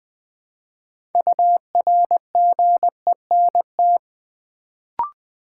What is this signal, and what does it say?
Morse code, heard as a keyed tone, about 20 wpm, 700 Hz. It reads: URGENT
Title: Build A Word - 3 Letters At A Time - 6-Letter Words - 20wpm